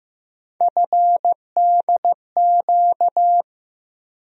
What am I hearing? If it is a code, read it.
Morse code, heard as a keyed tone, about 15 wpm, 700 Hz. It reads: FDQ
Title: USA Call Sign Postfixes - 3 Letters 15wpm